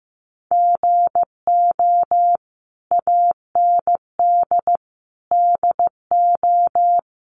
Morse code, heard as a keyed tone, about 15 wpm, 700 Hz. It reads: GO AND DO
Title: Sentences from Top 1100 Words 15wpm